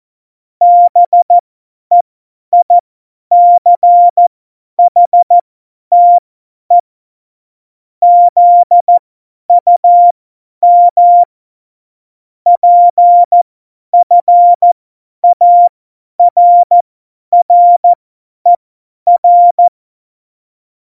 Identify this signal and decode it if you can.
Morse code, heard as a keyed tone, about 14 wpm, spaced as for 10 wpm, 700 Hz. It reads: BEICHTE ZUM PFARRER
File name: Morsekurs / Witze / Witz 20. Oktober 2020, 50 F70